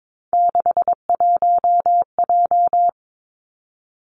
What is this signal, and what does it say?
Morse code, heard as a keyed tone, about 22 wpm, 700 Hz. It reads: 61J
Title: Three Challenging Letter-Number Combinations 22wpm